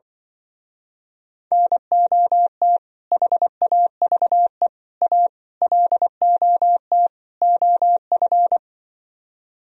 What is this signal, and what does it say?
Morse code, heard as a keyed tone, about 24 wpm, 700 Hz. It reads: NOT HAVE A LOT OF
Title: Speed-Racing - Sets of 5 Words 20wpm